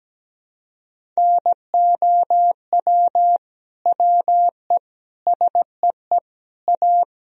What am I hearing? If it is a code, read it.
Morse code, heard as a keyed tone, about 17 wpm, 700 Hz. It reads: NOW WE SEE A
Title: Sentences from Top 1000 Words - Review Part 1 - 17wpm